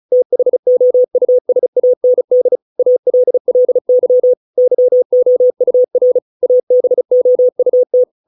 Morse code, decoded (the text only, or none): THOUSAND ALLY YOUR ABOUT